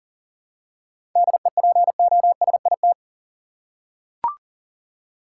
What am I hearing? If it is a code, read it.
Morse code, heard as a keyed tone, about 40 wpm, 700 Hz. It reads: DEPOSIT
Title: Seven Letter Words - 40wpm